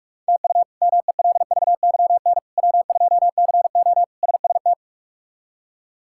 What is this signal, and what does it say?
Morse code, heard as a keyed tone, about 46 wpm, 700 Hz. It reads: TU MELVYN W2CO SST
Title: K1USN SST QSO Practice - 45wpm